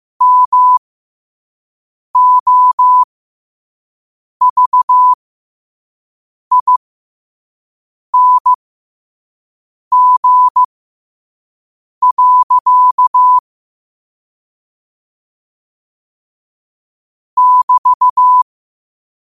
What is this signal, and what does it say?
Morse code, heard as a keyed tone, about 15 wpm, spaced as for 5 wpm, 1000 Hz. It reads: MOVING. <BT>
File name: qotdfiles / 5wpm